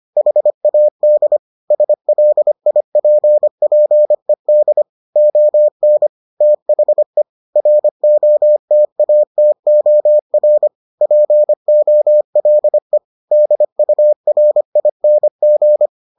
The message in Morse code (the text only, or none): HAD SLIPPED ON THE ROTATOR POLE DURING